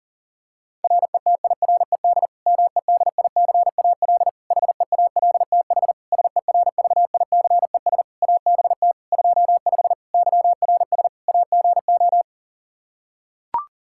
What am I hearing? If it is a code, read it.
Morse code, heard as a keyed tone, about 40 wpm, 700 Hz. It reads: RETIRED MEDICAL HEALTH SERVICES ABT 25 YRS AGO
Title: Practice QSO - Exchange 3 - More Info - 40wpm